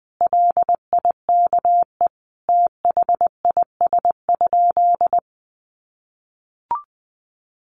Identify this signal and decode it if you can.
Morse code, heard as a keyed tone, about 20 wpm, 700 Hz. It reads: LIKE THIS?
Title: Sentences from Top 200 Words 20wpm